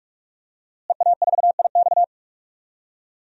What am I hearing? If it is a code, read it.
Morse code, heard as a keyed tone, about 42 wpm, 700 Hz. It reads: EA4IX
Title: DXCC-100 Call Signs - 5 Characters - Repeated 3x - 45wpm